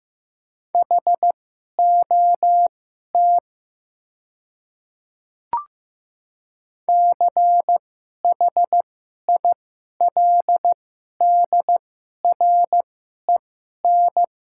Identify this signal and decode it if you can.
Morse code, heard as a keyed tone, about 15 wpm, spaced as for 11 wpm, 700 Hz. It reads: HOT CHILDREN
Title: 2x Character Spacing - Top 500 Words - 15wpm